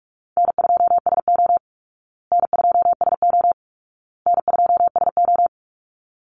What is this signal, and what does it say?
Morse code, heard as a keyed tone, about 44 wpm, 700 Hz. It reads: N2SO N2SO N2SO
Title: Call Signs Repeated 2x - 45wpm